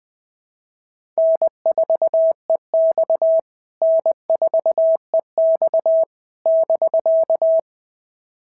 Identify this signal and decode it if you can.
Morse code, heard as a keyed tone, about 20 wpm, 690 Hz. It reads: N4EX N4EX <BK>
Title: POTA QSO Practice - 20wpm